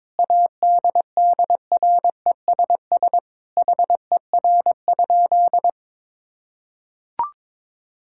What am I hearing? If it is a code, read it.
Morse code, heard as a keyed tone, about 22 wpm, 700 Hz. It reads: ADDRESS HER?
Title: Sentences from Top 1000 Words - Review Part 6 - 22wpm